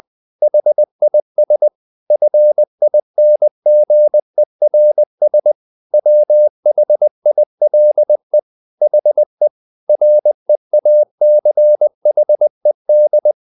Morse code, heard as a keyed tone, about 20 wpm, 600 Hz. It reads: HIS FINGERS WHILE HE REACHED
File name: the old man and the sea / the old man and the sea part 1 20wpm